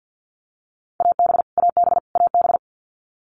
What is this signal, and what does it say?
Morse code, heard as a keyed tone, about 50 wpm, 700 Hz. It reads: AB AB AB
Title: Canadian Provinces and Territories - Repeated 3x - 50wpm